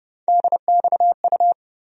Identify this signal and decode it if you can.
Morse code, heard as a keyed tone, about 30 wpm, 700 Hz. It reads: DXU
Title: USA Call Sign Postfixes - 3 Letters 30wpm